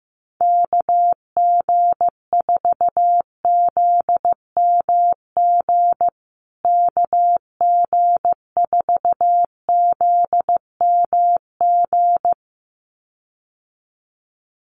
Morse code, heard as a keyed tone, about 15 wpm, 700 Hz. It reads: KG4ZMG KG4ZMG
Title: USA Call Signs Repeated 2x 15wpm